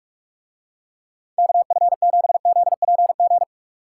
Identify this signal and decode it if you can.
Morse code, heard as a keyed tone, about 44 wpm, 700 Hz. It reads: KF7ZPG